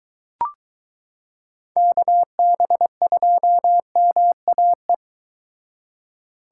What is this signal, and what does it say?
Morse code, heard as a keyed tone, about 23 wpm, 700 Hz. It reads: KB2MAE